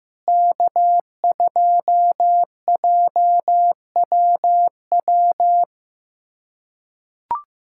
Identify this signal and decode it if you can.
Morse code, heard as a keyed tone, about 15 wpm, 700 Hz. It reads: K2JWW
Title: SOTA QSO Practice 15wpm